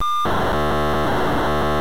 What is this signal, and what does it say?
POCSAG pager data.